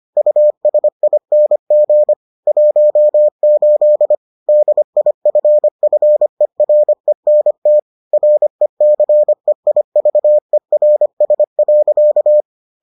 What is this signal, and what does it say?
Morse code, heard as a keyed tone, about 25 wpm, 600 Hz. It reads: USING 18 DIFFERENT RECEIVERS.